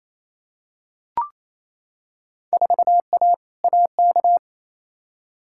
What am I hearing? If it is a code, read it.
Morse code, heard as a keyed tone, about 28 wpm, 700 Hz. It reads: 4A AK
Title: Field Day Exchanges Encore - 28wpm